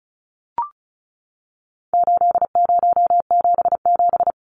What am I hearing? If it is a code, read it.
Morse code, heard as a keyed tone, about 34 wpm, 700 Hz. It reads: 8077
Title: Numbers - 4-Digit 35wpm